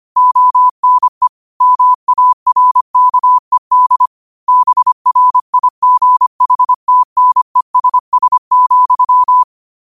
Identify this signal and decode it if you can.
Morse code, heard as a keyed tone, about 25 wpm, 1000 Hz. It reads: ONE MARKED BRIGHTNESS,
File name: qotdfiles / 25wpm